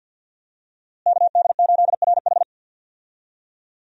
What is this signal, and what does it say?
Morse code, heard as a keyed tone, about 52 wpm, 700 Hz. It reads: KD8RH
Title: USA Call Signs - 5 Characters - Repeated 2x - 50wpm